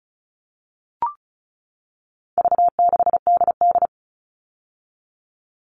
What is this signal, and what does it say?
Morse code, heard as a keyed tone, about 34 wpm, 700 Hz. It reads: V6DD
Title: All Cut Numbers - 4-Digit 35wpm